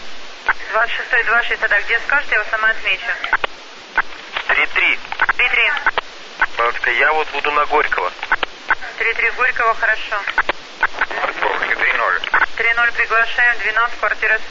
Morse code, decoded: none

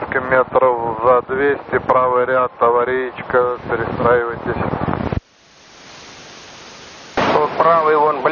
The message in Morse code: none

